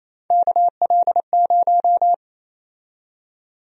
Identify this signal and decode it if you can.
Morse code, heard as a keyed tone, about 28 wpm, 700 Hz. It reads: KL0
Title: USA Call Sign Prefixes - 28wpm